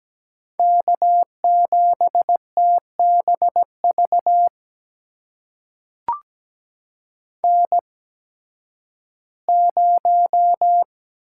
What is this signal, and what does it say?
Morse code, heard as a keyed tone, about 17 wpm, spaced as for 11 wpm, 700 Hz. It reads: K7TBV N0
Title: USA Call Signs - 5 Characters - 8x Character Spacing - 17wpm